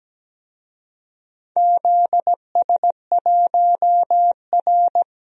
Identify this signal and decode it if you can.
Morse code, heard as a keyed tone, about 17 wpm, 700 Hz. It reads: ZS1R
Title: DXCC-100 Call Signs - 4 Characters - 17wpm